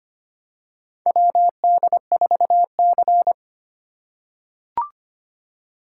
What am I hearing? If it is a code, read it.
Morse code, heard as a keyed tone, about 25 wpm, 700 Hz. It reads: WD4C
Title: USA Call Signs - 4 Characters - 6x Character Spacing - 25wpm